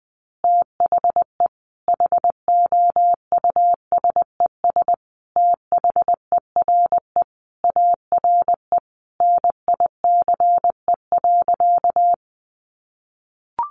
Morse code, heard as a keyed tone, about 20 wpm, 700 Hz. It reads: THE HOUSES THERE ARE NICE.
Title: Easy Phrases 20wpm